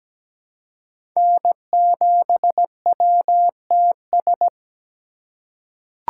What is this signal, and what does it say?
Morse code, heard as a keyed tone, about 17 wpm, 700 Hz. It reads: N7WTS
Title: USA Call Signs - 2x Character Spacing - 17wpm